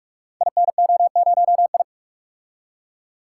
Morse code, heard as a keyed tone, about 44 wpm, 700 Hz. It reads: INO0I
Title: Code Groups - 5 Letter:Numbers - 45wpm